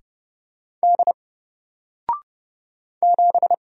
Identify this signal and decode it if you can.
Morse code, heard as a keyed tone, about 31 wpm, 700 Hz. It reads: D 7
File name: Single-Letter-Number-30wpm